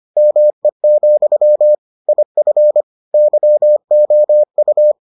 Morse code, heard as a keyed tone, about 25 wpm, 600 Hz. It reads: ME, IF YOU